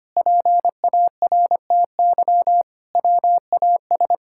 Morse code, heard as a keyed tone, about 25 wpm, 700 Hz. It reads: PARTY WAS
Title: Sentences from Top 1600 Words 25wpm